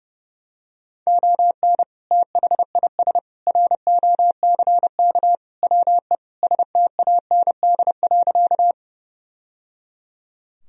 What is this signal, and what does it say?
Morse code, heard as a keyed tone, about 30 wpm, 700 Hz. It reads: ON THIS ROCK WE STAND.
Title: Sentences from Top 1000 Words - Review Part 2 - 30wpm